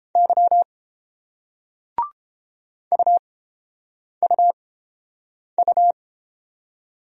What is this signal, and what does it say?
Morse code, heard as a keyed tone, about 31 wpm, spaced as for 8 wpm, 700 Hz. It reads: Y UUU